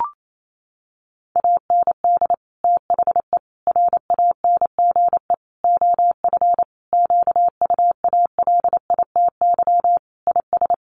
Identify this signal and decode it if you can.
Morse code, heard as a keyed tone, about 28 wpm, 700 Hz. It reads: AND THE RANGE OF QUALITY IS
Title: Sentences from Top 1500 Words 28wpm